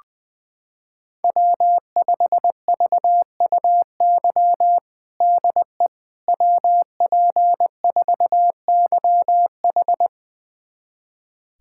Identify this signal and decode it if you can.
Morse code, heard as a keyed tone, about 21 wpm, 700 Hz. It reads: W5VUY DE WP4YH
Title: USA Callsign DE Callsign - 5 Characters - 20wpm